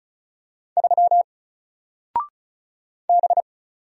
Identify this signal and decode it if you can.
Morse code, heard as a keyed tone, about 36 wpm, 700 Hz. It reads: 3 B
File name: Single-Letter-Number-35wpm